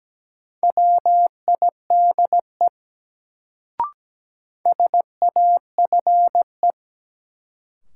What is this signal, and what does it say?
Morse code, heard as a keyed tone, about 17 wpm, 700 Hz. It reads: WIDE SAFE
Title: Sentences from Top 900 Words - New Word Drill - 17wpm